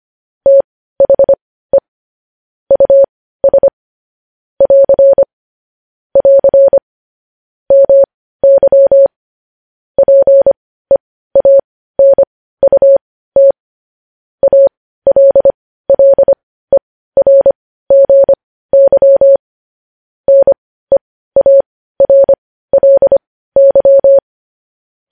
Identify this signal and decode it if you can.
Morse code, heard as a keyed tone, about 25 wpm, spaced as for 15 wpm, 550 Hz. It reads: THE US <AR> <AR> MY PEANUT ALLERGY NEARLY
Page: news headlines in morse code at 15 wpm